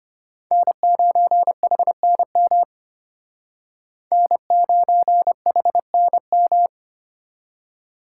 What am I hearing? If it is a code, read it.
Morse code, heard as a keyed tone, about 32 wpm, 700 Hz. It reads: N9HNM N9HNM